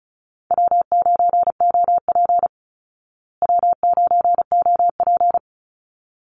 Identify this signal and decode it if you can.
Morse code, heard as a keyed tone, about 34 wpm, 700 Hz. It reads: W9OP W9OP
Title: Call Signs Repeated 3x - 35wpm